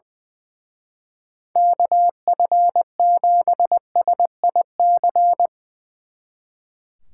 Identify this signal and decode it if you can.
Morse code, heard as a keyed tone, about 20 wpm, 700 Hz. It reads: KF7SIC
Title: Speed Racing - USA Call Signs 20wpm